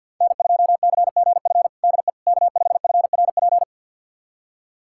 Morse code, heard as a keyed tone, about 52 wpm, 700 Hz. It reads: N2CGF DE K5FRP
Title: USA Callsign DE Callsign - 5 Characters - 50wpm